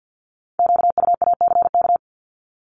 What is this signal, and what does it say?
Morse code, heard as a keyed tone, about 50 wpm, 700 Hz. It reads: QUACK
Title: Multi-letter Phonemes 50wpm